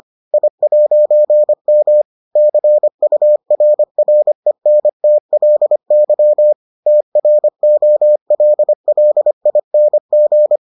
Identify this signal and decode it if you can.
Morse code, heard as a keyed tone, about 25 wpm, 600 Hz. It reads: I'M CURRENTLY TROLLING